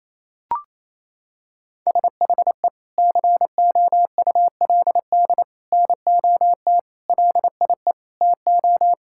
Morse code, heard as a keyed tone, about 28 wpm, 700 Hz. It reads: SHE COULD NOT LIE TO
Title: Sentences from Top 1600 Words 28wpm